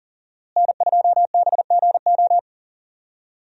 Morse code, heard as a keyed tone, about 40 wpm, 700 Hz. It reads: N2BGO
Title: USA Call Signs - 5 Characters - Repeated 3x - 40wpm